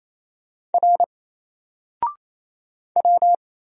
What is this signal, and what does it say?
Morse code, heard as a keyed tone, about 27 wpm, 700 Hz. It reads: R W
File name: Single-Letter-Number-28wpm